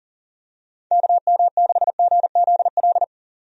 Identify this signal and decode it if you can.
Morse code, heard as a keyed tone, about 40 wpm, 700 Hz. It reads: KM6GZL
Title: USA Call Signs - 6 Characters - 40wpm